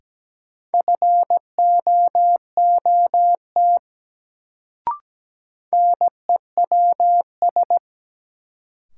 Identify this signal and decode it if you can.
Morse code, heard as a keyed tone, about 17 wpm, 700 Hz. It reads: FOOT NEWS
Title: Sets of 5 Words - New Word Drill - 17wpm